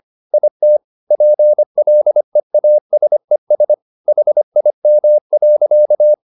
Morse code, heard as a keyed tone, about 25 wpm, 600 Hz. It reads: IT PLEASES HIM.